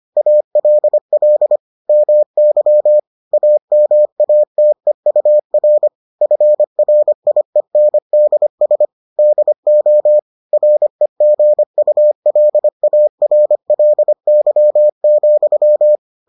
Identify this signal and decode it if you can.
Morse code, heard as a keyed tone, about 25 wpm, 600 Hz. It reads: ALL MY AMATEUR FRIENDS DO REGULARLY,